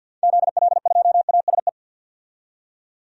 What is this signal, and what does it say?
Morse code, heard as a keyed tone, about 52 wpm, 700 Hz. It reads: ZL2ASE
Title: DXCC-100 Call Signs - 6 Characters - 50wpm